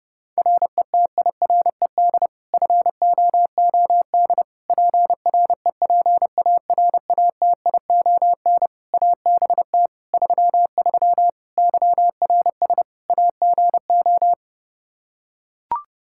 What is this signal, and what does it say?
Morse code, heard as a keyed tone, about 30 wpm, 700 Hz. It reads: RETIRED FOOD PREPARATION ABT 33 YRS AGO
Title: K1USN SST QSO Practice 30wpm